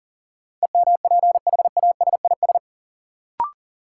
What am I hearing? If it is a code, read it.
Morse code, heard as a keyed tone, about 37 wpm, 700 Hz. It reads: EMPHASIS
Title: Sentences from Top 1400 Words - New Word Drill Alt - 40wpm